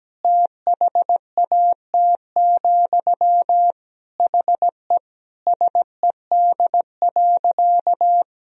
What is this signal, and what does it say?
Morse code, heard as a keyed tone, about 17 wpm, 700 Hz. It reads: THAT, HE SED.
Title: Sentences from Top 200 Words and Abbreviations 17wpm